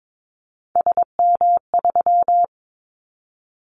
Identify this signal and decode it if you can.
Morse code, heard as a keyed tone, about 22 wpm, 700 Hz. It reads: SM3
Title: Callsign Prefixes for Top 25 DXCC - Repeated 3x - 22wpm